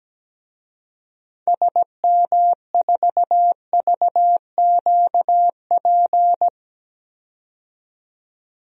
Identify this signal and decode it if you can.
Morse code, heard as a keyed tone, about 17 wpm, 700 Hz. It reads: SM4VQP
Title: DXCC-100 Call Signs - 6 Characters - 17wpm